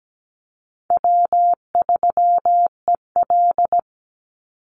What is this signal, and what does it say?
Morse code, heard as a keyed tone, about 17 wpm, 700 Hz. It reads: W3EL
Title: Call Signs Repeated 2x - 17wpm